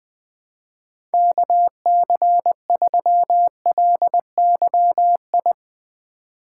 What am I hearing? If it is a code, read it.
Morse code, heard as a keyed tone, about 20 wpm, 700 Hz. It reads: KC3LYI